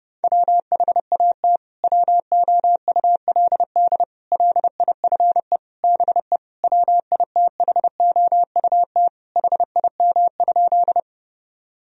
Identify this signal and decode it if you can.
Morse code, heard as a keyed tone, about 32 wpm, 700 Hz. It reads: WHAT WOULD LIFE BE WITHOUT HIM?
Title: Sentences from Top 1000 Words - Review Part 3 - 30wpm